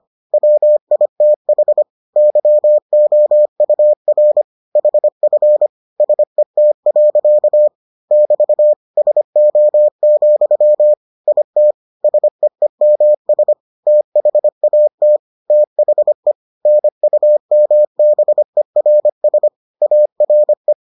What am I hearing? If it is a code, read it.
Morse code, heard as a keyed tone, about 25 wpm, 600 Hz. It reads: WITH YOUR HF SET. <BT> SO, IT SEEMS THAT THE NUMBERS ARE